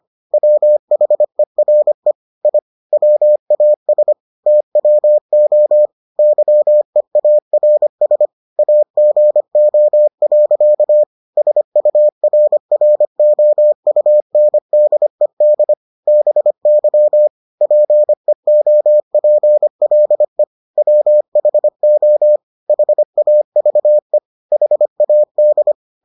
Morse code, heard as a keyed tone, about 25 wpm, 600 Hz. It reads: WHERE I WAS TWO YEARS AGO. SURROUNDED BY PEOPLE WHO HAVE HAD